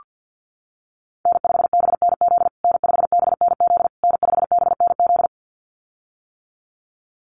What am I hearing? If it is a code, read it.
Morse code, heard as a keyed tone, about 48 wpm, 700 Hz. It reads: N5BNZ N5BNZ N5BNZ
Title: USA Call Signs Repeated 3x 50wpm